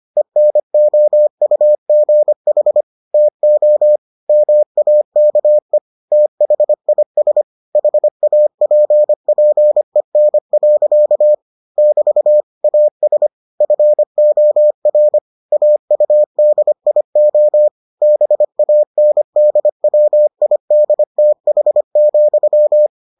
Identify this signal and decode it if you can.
Morse code, heard as a keyed tone, about 25 wpm, 600 Hz. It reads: ENOUGH TO MAKE THIS HAPPEN. <BT> AS FOR AUDIO BANDWIDTH,